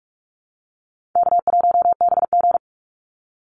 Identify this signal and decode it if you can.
Morse code, heard as a keyed tone, about 44 wpm, 700 Hz. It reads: K1BG
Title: Easy Call Signs Repeated 3x - 45wpm